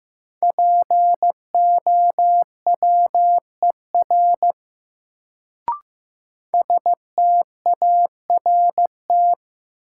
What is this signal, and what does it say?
Morse code, heard as a keyed tone, about 15 wpm, 700 Hz. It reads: POWER START
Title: Sets of 4 Words - New Word Drill - 15wpm